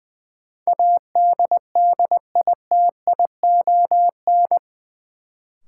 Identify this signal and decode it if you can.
Morse code, heard as a keyed tone, about 20 wpm, 700 Hz. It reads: ADDITION
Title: Sentences from Top 1500 Words - New Word Drill Alt - 20wpm